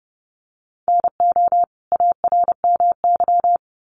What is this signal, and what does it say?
Morse code, heard as a keyed tone, about 30 wpm, 700 Hz. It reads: NO ARMY
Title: Sentences from Top 1300 Words 30wpm